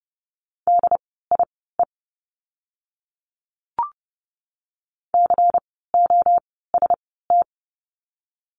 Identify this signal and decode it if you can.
Morse code, heard as a keyed tone, about 30 wpm, spaced as for 17 wpm, 700 Hz. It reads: DIE COST
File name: Lesson 16f - Letter c - Words 30wpm